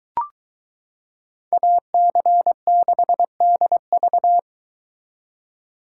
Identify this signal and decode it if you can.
Morse code, heard as a keyed tone, about 22 wpm, 700 Hz. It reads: AC6DV